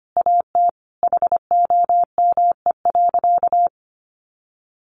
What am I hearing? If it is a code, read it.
Morse code, heard as a keyed tone, about 25 wpm, 700 Hz. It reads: AT HOME.
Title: Sentences from Top 600 Words 25wpm